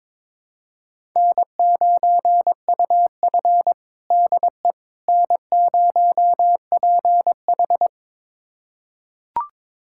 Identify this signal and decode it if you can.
Morse code, heard as a keyed tone, about 22 wpm, 700 Hz. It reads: N9UF DE N0PH
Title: USA Callsign DE Callsign - 4 Characters - 22wpm